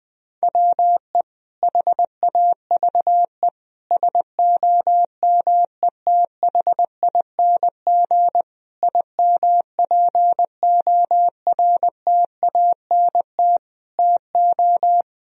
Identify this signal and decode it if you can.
Morse code, heard as a keyed tone, about 21 wpm, 700 Hz. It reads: WE HAVE SOMETHING IMPORTANT TO
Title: Sentences from Top 1000 Words - Review Part 4 - 20wpm